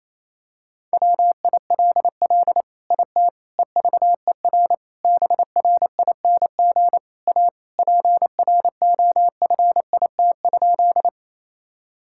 Morse code, heard as a keyed tone, about 28 wpm, 700 Hz. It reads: WILL IT EVER BRING A PROFIT?
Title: Sentences from Top 1000 Words - Review Part 5 - 28wpm